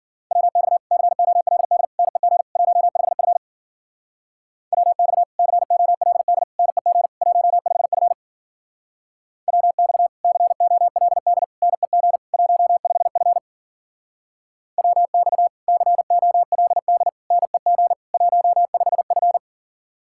Morse code, heard as a keyed tone, about 52 wpm, 700 Hz. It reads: WX COLD NEG 15F WX COLD NEG 15F WX COLD NEG 15F WX COLD NEG 15F